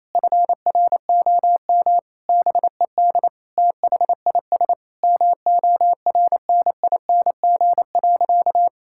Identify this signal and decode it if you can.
Morse code, heard as a keyed tone, about 27 wpm, 700 Hz. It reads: FROM BED THIS MORNING.
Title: Sentences from Top 1000 Words - Review Part 6 - 28wpm